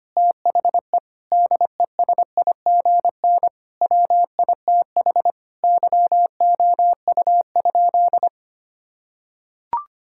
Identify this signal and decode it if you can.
Morse code, heard as a keyed tone, about 25 wpm, 700 Hz. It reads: THE DESIGN WITH YOU?
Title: Sentences from Top 1000 Words - Review Part 5 - 25wpm